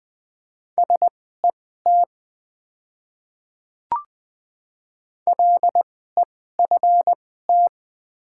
Morse code, heard as a keyed tone, about 20 wpm, spaced as for 14 wpm, 700 Hz. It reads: SET LEFT
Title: 2x Character Spacing - Top 300 Words - 20wpm